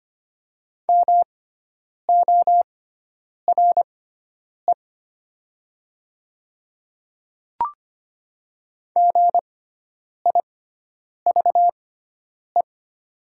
Morse code, heard as a keyed tone, about 25 wpm, spaced as for 9 wpm, 700 Hz. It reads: MORE GIVE
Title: 6x Character Spacing - Top 300 Words - 25wpm